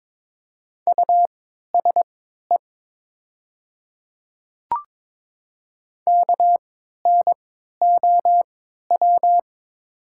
Morse code, heard as a keyed tone, about 21 wpm, spaced as for 12 wpm, 700 Hz. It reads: USE KNOW
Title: 3x Character Spacing - Top 200 Words - 22wpm